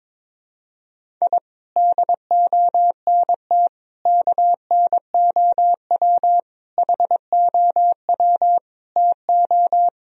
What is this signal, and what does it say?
Morse code, heard as a keyed tone, about 21 wpm, 700 Hz. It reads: I DONT KNOW HOW TO
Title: Sentences from Top 1000 Words - Review Part 1 - 22wpm